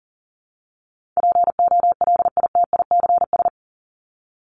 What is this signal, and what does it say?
Morse code, heard as a keyed tone, about 40 wpm, 700 Hz. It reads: POLITICS
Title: Top 751-1000 Words 40wpm